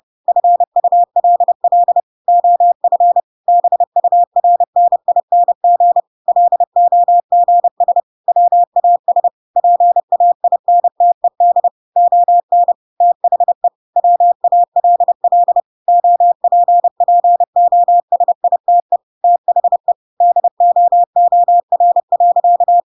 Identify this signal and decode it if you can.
Morse code, heard as a keyed tone, about 30 wpm, 700 Hz. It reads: FULL OF BURNING LOGS WAS PAINTED ON THE WALL OPPOSITE THE DOOR.
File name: Pinocchio / Pinocchio ch3 30 30 ik1hge